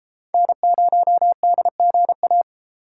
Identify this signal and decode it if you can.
Morse code, heard as a keyed tone, about 33 wpm, 700 Hz. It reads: N0DGA